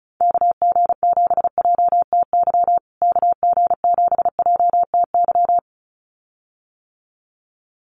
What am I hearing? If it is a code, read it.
Morse code, heard as a keyed tone, about 34 wpm, 700 Hz. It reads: KG7JTY KG7JTY
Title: USA Call Signs Repeated 2x 35wpm